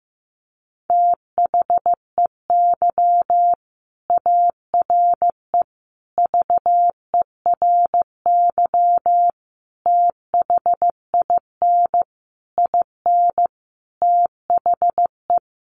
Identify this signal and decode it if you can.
Morse code, heard as a keyed tone, about 15 wpm, 700 Hz. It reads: THEY ARE VERY THIN IN THE
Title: Sentences from Top 1200 Words 15wpm